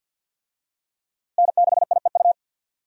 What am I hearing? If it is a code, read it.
Morse code, heard as a keyed tone, about 52 wpm, 700 Hz. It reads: N6IEU
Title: USA Call Signs - 5 Characters - Repeated 2x - 50wpm